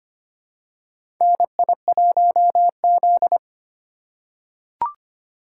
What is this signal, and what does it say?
Morse code, heard as a keyed tone, about 25 wpm, 700 Hz. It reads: NI1Z
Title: USA Call Signs - 4 Characters - 8x Character Spacing - 25wpm